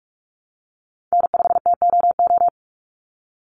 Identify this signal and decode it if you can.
Morse code, heard as a keyed tone, about 44 wpm, 700 Hz. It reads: N5TOO
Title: Call Signs Repeated 2x - 45wpm